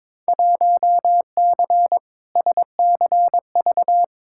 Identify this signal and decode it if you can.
Morse code, heard as a keyed tone, about 22 wpm, 700 Hz. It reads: 1C SCV
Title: Field Day Exchanges - 22wpm